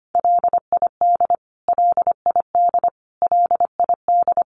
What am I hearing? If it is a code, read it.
Morse code, heard as a keyed tone, about 25 wpm, 700 Hz. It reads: LID LID LID
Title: Abbreviations Repeated 3x - 25wpm